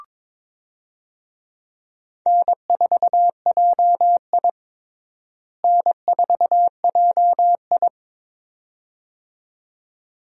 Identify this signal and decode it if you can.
Morse code, heard as a keyed tone, about 22 wpm, 700 Hz. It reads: N4JI N4JI
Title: USA Call Signs - 4 Characters - Repeated 2x - 22wpm